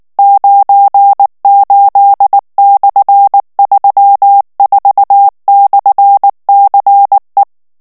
Morse code, heard as a keyed tone, about 19 wpm, 800 Hz. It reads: 98/34/CE
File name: morse mp3 / ESPAGNOL TEXTE EN CLAIR 2 VITESSE 18 MOTS MINUTES